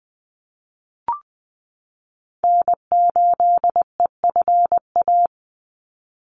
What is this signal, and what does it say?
Morse code, heard as a keyed tone, about 20 wpm, 700 Hz. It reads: N8EFA
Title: USA Call Signs 20wpm